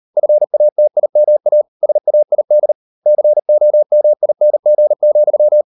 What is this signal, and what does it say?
Morse code, heard as a keyed tone, about 38 wpm, 600 Hz. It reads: FATIMA SAID COMING,